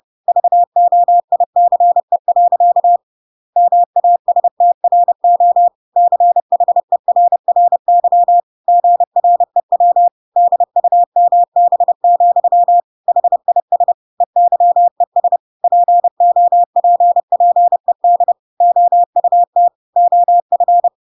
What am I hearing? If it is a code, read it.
Morse code, heard as a keyed tone, about 30 wpm, 700 Hz. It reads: VOICE. MASTRO CHERRY GREW DUMB, HIS EYES POPPED OUT OF
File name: Pinocchio / Pinocchio ch1 30 30 ik1hge